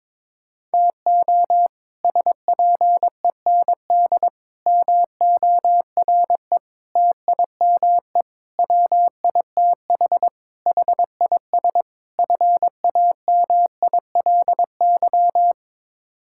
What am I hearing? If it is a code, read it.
Morse code, heard as a keyed tone, about 22 wpm, 700 Hz. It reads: TO SPEND MORE TIME WITH HIS FAMILY
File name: practice / 22wpm / Sets of 7 Words - 22wpm